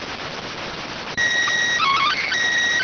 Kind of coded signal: SSTV (slow-scan television)

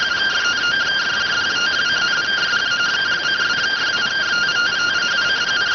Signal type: two-tone teleprinter signal

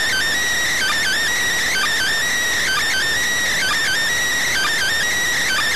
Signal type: Emergency Alert System (SAME) tones

